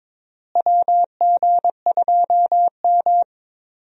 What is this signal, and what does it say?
Morse code, heard as a keyed tone, about 22 wpm, 700 Hz. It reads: WG2M
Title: USA Call Signs - 4 Characters - Repeated 2x - 22wpm